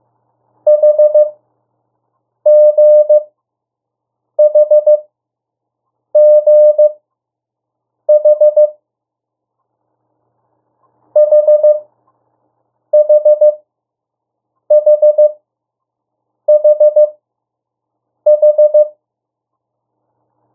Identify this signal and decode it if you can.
Morse code, heard as a keyed tone, about 15 wpm, spaced as for 6 wpm, 600 Hz. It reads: HGHGH HHHHH
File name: morse mp3 / 02 urok 1